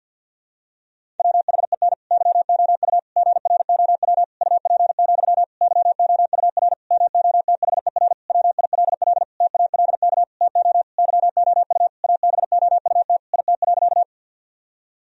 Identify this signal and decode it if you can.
Morse code, heard as a keyed tone, about 47 wpm, 700 Hz. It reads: WHEN YOU GROW UP, YOUR MOTHER WILL TALK TO YOU ABOUT IT.
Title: Sentences from Top 1000 Words - Review Part 1 - 50wpm